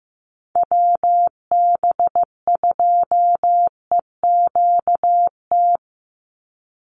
Morse code, heard as a keyed tone, about 15 wpm, 700 Hz. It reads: WB2EQT
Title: USA Call Signs 15wpm